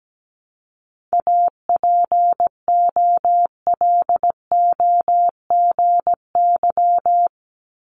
Morse code, heard as a keyed tone, about 17 wpm, 700 Hz. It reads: APOLOGY
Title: Multi-letter Phonemes 17wpm